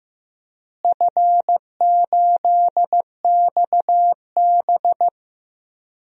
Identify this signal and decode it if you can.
Morse code, heard as a keyed tone, about 15 wpm, 700 Hz. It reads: F8XB
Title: DXCC-100 Call Signs - 4 Characters - Repeated 3x - 15wpm